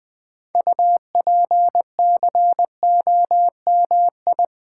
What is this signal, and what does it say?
Morse code, heard as a keyed tone, about 20 wpm, 700 Hz. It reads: UPCOMI
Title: Build A Word - 2 Letters At A Time - 8-Letter Words - 20wpm